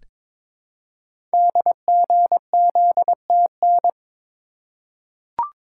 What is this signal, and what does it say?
Morse code, heard as a keyed tone, about 22 wpm, 700 Hz. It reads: DGZTN
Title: Code Groups - 5 Letter:Numbers - 22wpm